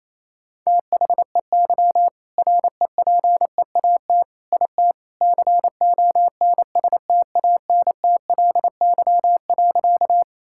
Morse code, heard as a keyed tone, about 28 wpm, 700 Hz. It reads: THEY REPEAT IT CONSTANTLY.
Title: Sentences from Top 1600 Words 28wpm